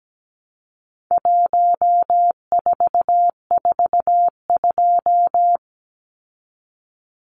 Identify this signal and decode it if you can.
Morse code, heard as a keyed tone, about 17 wpm, 700 Hz. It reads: 1442
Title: Numbers - 4-Digit 17wpm